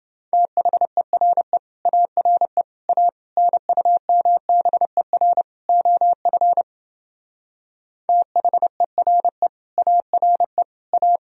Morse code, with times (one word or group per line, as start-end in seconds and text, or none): 0.3-1.6 THERE
1.9-2.6 ARE
2.9-3.1 A
3.4-5.4 NUMBER
5.7-6.6 OF
8.1-9.5 THERE
9.8-10.6 ARE
10.9-11.2 A